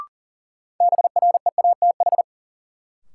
Morse code, heard as a keyed tone, about 40 wpm, 700 Hz. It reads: BREATH
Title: Sets of 5 Words Encore - New Word Drill - 40wpm